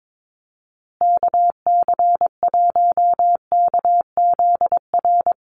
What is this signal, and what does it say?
Morse code, heard as a keyed tone, about 22 wpm, 700 Hz. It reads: KC1KZR
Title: USA Call Signs Repeated 2x 22wpm